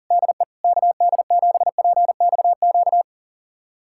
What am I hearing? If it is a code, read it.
Morse code, heard as a keyed tone, about 40 wpm, 700 Hz. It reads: DE KD7PXQ
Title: USA Callsign DE Callsign - 6 Characters - 40wpm